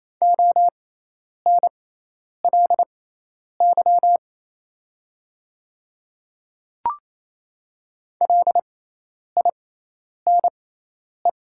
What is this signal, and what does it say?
Morse code, heard as a keyed tone, about 28 wpm, spaced as for 10 wpm, 700 Hz. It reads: ONLY LINE
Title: 6x Character Spacing - Top 300 Words - 28wpm